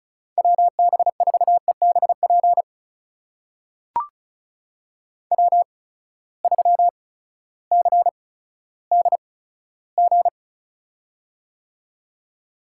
Morse code, heard as a keyed tone, about 36 wpm, spaced as for 16 wpm, 700 Hz. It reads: WB4EBP W3CDG
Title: USA Call Signs - 8x Character Spacing - 35wpm